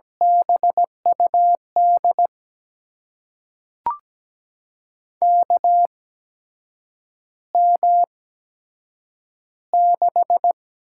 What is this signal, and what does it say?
Morse code, heard as a keyed tone, about 17 wpm, spaced as for 7 wpm, 700 Hz. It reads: BUD KM6
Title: USA Call Signs - 6 Characters - 8x Character Spacing - 17wpm